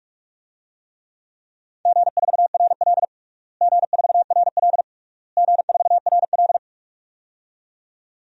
Morse code, heard as a keyed tone, about 42 wpm, 700 Hz. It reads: G4RL G4RL G4RL
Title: DXCC-100 Call Signs - 4 Characters - Repeated 3x - 45wpm